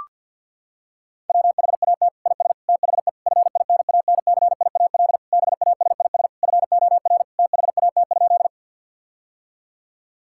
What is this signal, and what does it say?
Morse code, heard as a keyed tone, about 52 wpm, 700 Hz. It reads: WHAT IS THE FINANCIAL BASIS FOR THAT?
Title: Sentences from Top 1000 Words - Review Part 5 - 50wpm